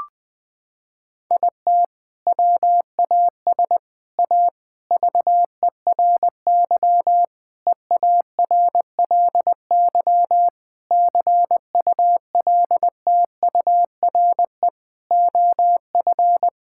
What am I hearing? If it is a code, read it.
Morse code, heard as a keyed tone, about 20 wpm, 700 Hz. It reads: IT WAS A VERY EARLY CULTURE OF
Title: Sentences from Top 1000 Words - Review Part 6 - 20wpm